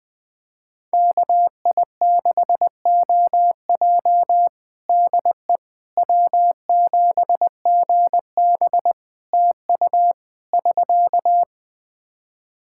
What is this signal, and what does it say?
Morse code, heard as a keyed tone, about 20 wpm, 700 Hz. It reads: KI6OJ DE W7GB TU <SK>
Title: Practice QSO - Exchange 4 - Ending - 20wpm